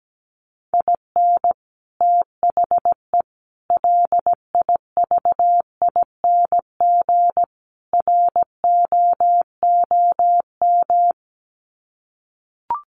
Morse code, heard as a keyed tone, about 17 wpm, 700 Hz. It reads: IN THE LIVING ROOM
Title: Sets of 4 Words 17wpm